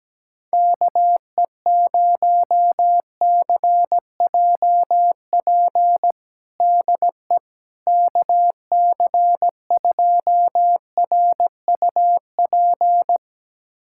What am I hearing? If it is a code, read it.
Morse code, heard as a keyed tone, about 17 wpm, 700 Hz. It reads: KE0CJP DE KC2RUP
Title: USA Callsign DE Callsign - 6 Characters - 17wpm